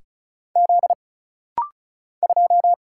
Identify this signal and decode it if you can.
Morse code, heard as a keyed tone, about 34 wpm, 700 Hz. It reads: Z 2
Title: Single Letter-Number - Rapid-Fire 35wpm